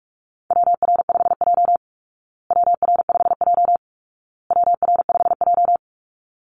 Morse code, heard as a keyed tone, about 47 wpm, 700 Hz. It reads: WR5J WR5J WR5J
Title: Call Signs Repeated 2x - 45wpm